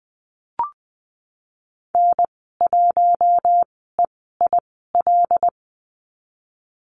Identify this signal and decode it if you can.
Morse code, heard as a keyed tone, about 20 wpm, spaced as for 14 wpm, 700 Hz. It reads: N1EIL
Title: 082 Lesson 14g - Number 2 - Call Signs 20wpm